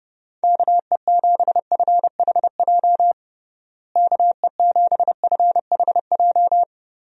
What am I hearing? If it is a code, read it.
Morse code, heard as a keyed tone, about 30 wpm, 700 Hz. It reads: KE7FHJ KE7FHJ
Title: USA Call Signs - 6 Characters - Repeated 2x - 30wpm